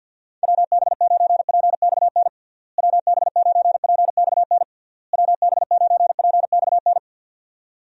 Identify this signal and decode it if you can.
Morse code, heard as a keyed tone, about 50 wpm, 700 Hz. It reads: WB9PXN WB9PXN WB9PXN
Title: USA Call Signs - 6 Characters - Repeated 3x - 50wpm